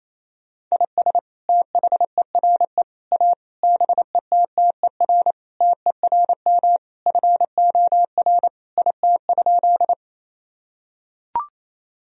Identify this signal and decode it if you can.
Morse code, heard as a keyed tone, about 28 wpm, 700 Hz. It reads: IS THERE A BETTER TERM FOR IT?
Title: Sentences from Top 1000 Words - Review Part 4 - 28wpm